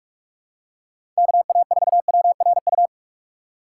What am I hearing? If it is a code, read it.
Morse code, heard as a keyed tone, about 46 wpm, 700 Hz. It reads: KA4WRU
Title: USA Call Signs - 6 Characters - 4x Character Spacing - 45wpm